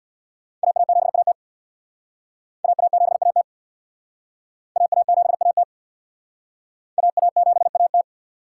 Morse code, heard as a keyed tone, about 64 wpm, 690 Hz. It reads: AA7AT AA7AT AA7AT AA7AT